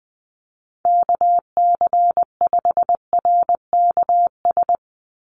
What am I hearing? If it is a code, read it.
Morse code, heard as a keyed tone, about 20 wpm, 700 Hz. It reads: KC5RKS
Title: USA Call Signs 20wpm